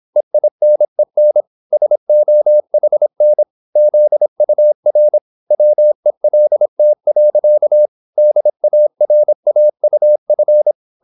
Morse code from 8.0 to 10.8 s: DARAUF